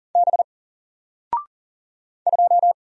Morse code, heard as a keyed tone, about 40 wpm, 700 Hz. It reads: B 2
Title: Single Character (Letter:Number:Punctuation:Prosign) 40wpm